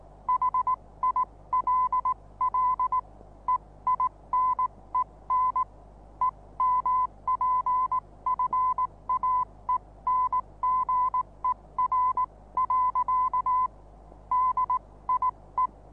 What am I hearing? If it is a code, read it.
Morse code, heard as a keyed tone, about 19 wpm, spaced as for 16 wpm, 1010 Hz. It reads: HILL EINEN EMPFAENGER. DIE
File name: repos / Morsekurs / DJ4UF Morsen26